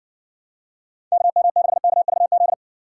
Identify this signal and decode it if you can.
Morse code, heard as a keyed tone, about 60 wpm, 700 Hz. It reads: KM6KVZ